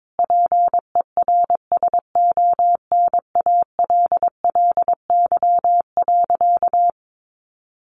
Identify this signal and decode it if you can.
Morse code, heard as a keyed tone, about 22 wpm, 700 Hz. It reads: PERSONALLY.
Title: Sentences from Top 1400 Words 22wpm